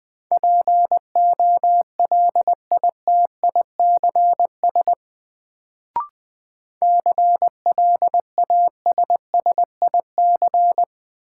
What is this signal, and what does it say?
Morse code, heard as a keyed tone, about 20 wpm, 700 Hz. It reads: POLITICS CLASSIC
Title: Sentences from Top 1000 Words - New Word Drill - 20wpm